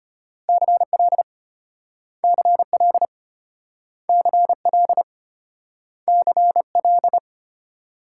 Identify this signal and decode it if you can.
Morse code, heard as a keyed tone, about 34 wpm, 700 Hz. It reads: CL CL CL CL